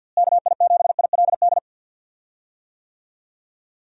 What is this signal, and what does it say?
Morse code, heard as a keyed tone, about 52 wpm, 700 Hz. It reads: KI7ILD
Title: USA Call Signs - 6 Characters - Repeated 3x - 50wpm